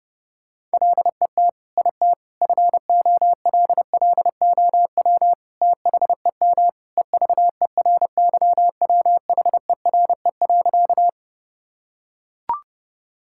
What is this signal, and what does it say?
Morse code, heard as a keyed tone, about 32 wpm, 700 Hz. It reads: LET IT FOLLOW THEM EVERYWHERE.
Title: Sentences from Top 1600 Words 30wpm